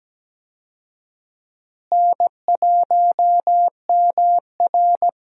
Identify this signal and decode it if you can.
Morse code, heard as a keyed tone, about 17 wpm, 700 Hz. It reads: N1MR
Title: USA Call Signs - 4 Characters - Repeated 2x - 17wpm